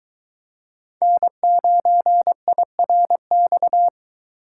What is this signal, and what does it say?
Morse code, heard as a keyed tone, about 24 wpm, 700 Hz. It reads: N9IRX